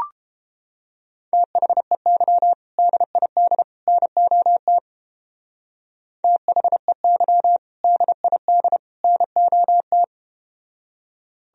Morse code, heard as a keyed tone, about 32 wpm, 700 Hz. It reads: THEY DID NOT THEY DID NOT